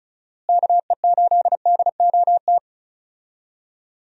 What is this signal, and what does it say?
Morse code, heard as a keyed tone, about 36 wpm, 700 Hz. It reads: KE8DOT
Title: USA Call Signs - 6 Characters - Repeated 3x - 35wpm